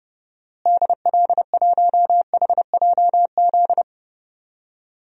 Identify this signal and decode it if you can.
Morse code, heard as a keyed tone, about 30 wpm, 700 Hz. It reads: DL1HJZ
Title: DXCC-100 Call Signs - 6 Characters - Repeated 3x - 30wpm